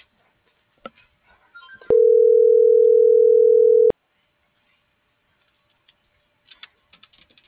Ringback tone (North America), 440 + 480 Hz.